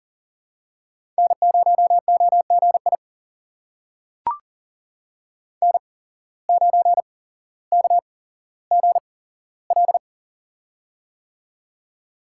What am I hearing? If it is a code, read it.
Morse code, heard as a keyed tone, about 39 wpm, spaced as for 17 wpm, 700 Hz. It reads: N0OGI N9KGL
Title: USA Call Signs - 5 Characters - 8x Character Spacing - 40wpm